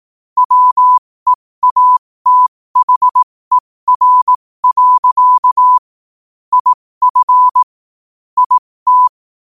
Morse code, heard as a keyed tone, about 18 wpm, spaced as for 15 wpm, 1000 Hz. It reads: WEATHER. IF IT
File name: qotdfiles / 15wpm